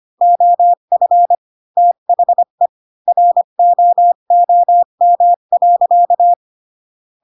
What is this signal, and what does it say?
Morse code, heard as a keyed tone, about 25 wpm, 700 Hz. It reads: OF THE ROOM.
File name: Pinocchio / Pinocchio ch7 20 25 ik1hge